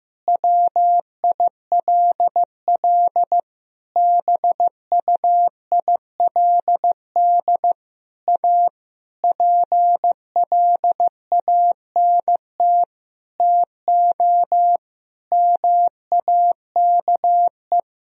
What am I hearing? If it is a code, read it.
Morse code, heard as a keyed tone, about 15 wpm, 700 Hz. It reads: WILL BUILD A PLANT TO MAKE
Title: Sentences from Top 1000 Words - Review Part 1 - 15wpm